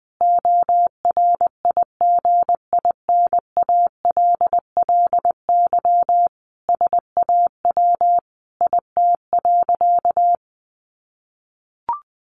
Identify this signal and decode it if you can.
Morse code, heard as a keyed tone, about 20 wpm, 700 Hz. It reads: ORIGINALLY SAW IT.
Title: Sentences from Top 1500 Words 20wpm